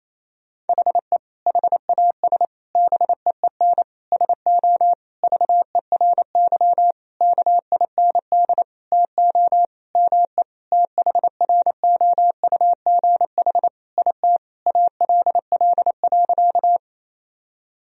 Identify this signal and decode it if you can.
Morse code, heard as a keyed tone, about 27 wpm, 700 Hz. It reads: HE HAS BEEN SO VERY KIND TO ME THROUGH IT ALL.
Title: Sentences from Top 1000 Words - Review Part 1 - 28wpm